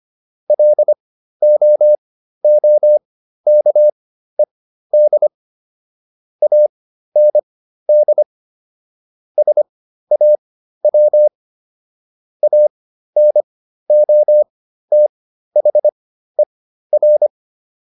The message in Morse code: LOOKED AND SAW ANOTHER

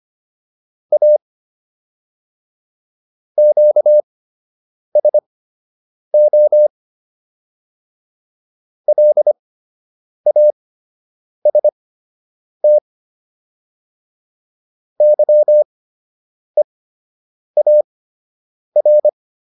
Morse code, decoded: A QSO LAST YEAR

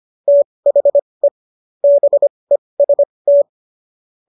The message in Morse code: THE BEST

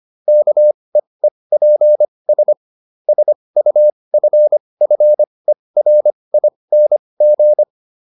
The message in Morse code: KEEPS SUFFERING

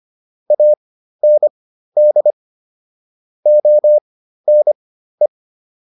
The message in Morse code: AND ONE

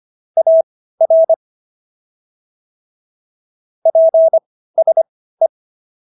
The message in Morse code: AR PSE

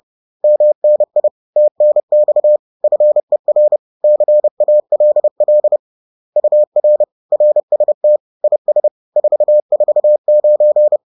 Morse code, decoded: MNI TNX FER CALL UR RST IS 449

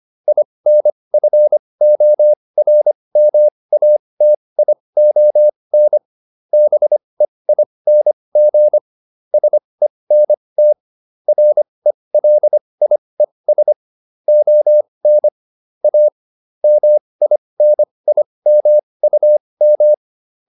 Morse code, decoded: INFORMATION BEING SENT RELIES ON A MINIMUM

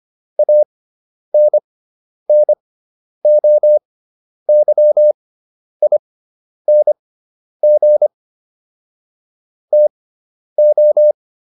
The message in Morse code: ANNOYING TO

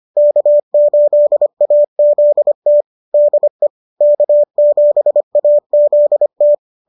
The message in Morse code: K8AZT DE K7AZT

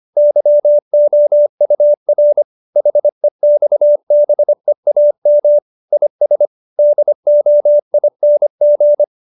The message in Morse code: YOUR HEXBEAM IS DOING

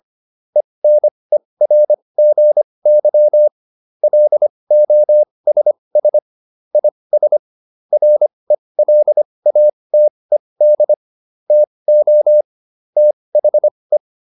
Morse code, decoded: ENERGY LOSS IS RELATED TO THE

